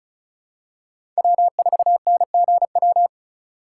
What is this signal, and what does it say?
Morse code, heard as a keyed tone, about 34 wpm, 700 Hz. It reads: W4NGW
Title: USA Call Signs - 4x Character Spacing - 35wpm